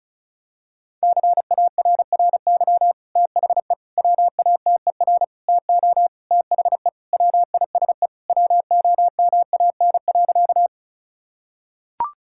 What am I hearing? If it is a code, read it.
Morse code, heard as a keyed tone, about 34 wpm, 700 Hz. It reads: CARRY THE WATER TO THE WISE WOMAN.
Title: Sentences from Top 1600 Words 35wpm